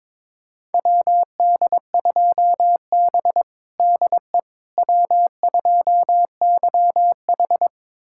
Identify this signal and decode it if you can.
Morse code, heard as a keyed tone, about 21 wpm, 700 Hz. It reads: WD2B DE W2YH
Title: USA Callsign DE Callsign - 4 Characters - 22wpm